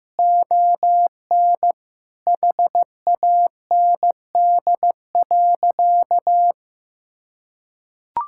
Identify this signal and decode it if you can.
Morse code, heard as a keyed tone, about 15 wpm, 700 Hz. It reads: ON HAND.
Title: Sentences from Top 1000 Words - Review Part 5 - 15wpm